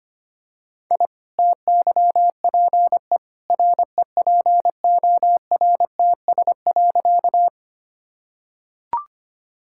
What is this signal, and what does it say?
Morse code, heard as a keyed tone, about 25 wpm, 690 Hz. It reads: I TYPE REPORTS.
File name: Easy Sentences with Hints - 25wpm